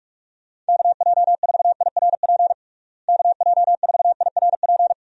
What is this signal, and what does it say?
Morse code, heard as a keyed tone, about 46 wpm, 700 Hz. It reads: KJ4IRP KJ4IRP
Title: USA Call Signs - 6 Characters - Repeated 3x - 45wpm